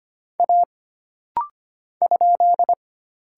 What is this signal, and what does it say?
Morse code, heard as a keyed tone, about 25 wpm, 700 Hz. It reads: A ?
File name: Single Character (Letter:Number:Punctuation:Prosign) - Rapid-Fire - 25wpm